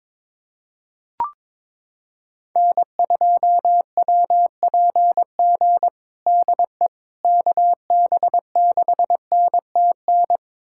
Morse code, heard as a keyed tone, about 22 wpm, 700 Hz. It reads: N2WPG DE KB6NTN
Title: USA Callsign DE Callsign 22wpm